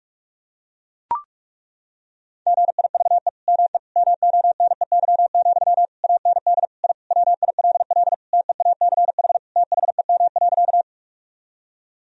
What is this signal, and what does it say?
Morse code, heard as a keyed tone, about 43 wpm, 700 Hz. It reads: GIVE ME MONEY, AND I WILL TEACH THEM.
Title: Sentences from Top 1000 Words - Review Part 2 - 45wpm